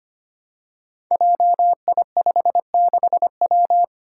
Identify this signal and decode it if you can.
Morse code, heard as a keyed tone, about 25 wpm, 700 Hz. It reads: JI56W
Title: Five Challenging Letter-Number Combinations 25wpm